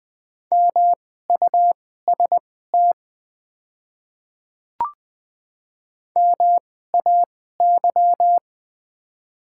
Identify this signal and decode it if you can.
Morse code, heard as a keyed tone, about 20 wpm, spaced as for 14 wpm, 700 Hz. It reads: MUST MAY
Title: 2x Character Spacing - Top 200 Words - 20wpm